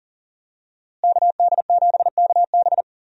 Morse code, heard as a keyed tone, about 40 wpm, 700 Hz. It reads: KD7KB
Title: USA Call Signs - 5 Characters - Repeated 2x - 40wpm